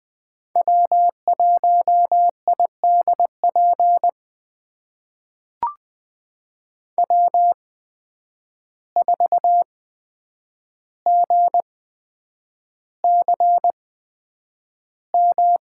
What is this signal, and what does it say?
Morse code, heard as a keyed tone, about 20 wpm, spaced as for 9 wpm, 700 Hz. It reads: W1IDP W4GCM
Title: USA Call Signs - 5 Characters - 8x Character Spacing - 20wpm